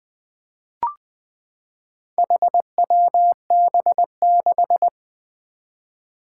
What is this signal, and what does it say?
Morse code, heard as a keyed tone, about 20 wpm, 700 Hz. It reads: HWB6
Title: Four Challenging Letter-Number Combinations 20wpm